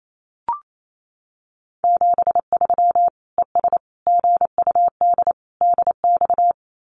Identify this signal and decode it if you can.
Morse code, heard as a keyed tone, about 28 wpm, 700 Hz. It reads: 73 ES GUD DX
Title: Common QSO Phrases 28wpm